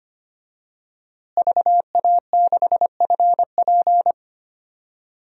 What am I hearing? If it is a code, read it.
Morse code, heard as a keyed tone, about 25 wpm, 700 Hz. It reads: VA6FP
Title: DXCC-100 Call Signs - 5 Characters - Repeated 2x - 25wpm